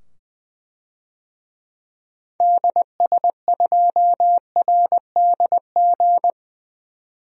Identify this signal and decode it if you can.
Morse code, heard as a keyed tone, about 20 wpm, 700 Hz. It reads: DS2RDG
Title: DXCC-100 Call Signs - 6 Characters - 20wpm